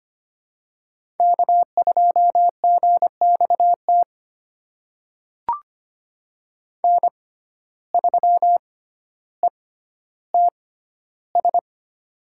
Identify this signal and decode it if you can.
Morse code, heard as a keyed tone, about 25 wpm, spaced as for 13 wpm, 700 Hz. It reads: K2GXT N3ETS
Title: USA Call Signs - 5 Characters - 6x Character Spacing - 25wpm